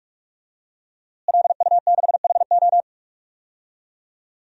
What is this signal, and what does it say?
Morse code, heard as a keyed tone, about 45 wpm, 700 Hz. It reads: LU6HO
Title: DXCC-100 Call Signs - 5 Characters - Repeated 3x - 45wpm